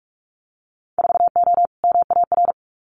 Morse code, heard as a keyed tone, about 44 wpm, 700 Hz. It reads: 4O MAR
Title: Winter Field Day Exchanges Encore 45wpm